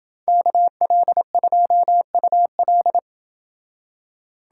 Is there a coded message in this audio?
Yes: Morse code from 0.3 to 3.0 s, heard as a keyed tone, about 30 wpm, 700 Hz. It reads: KL2UL